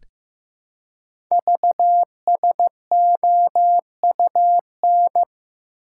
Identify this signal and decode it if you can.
Morse code, heard as a keyed tone, about 15 wpm, 700 Hz. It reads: VSOUN
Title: Code Groups - 5 Letters - 15wpm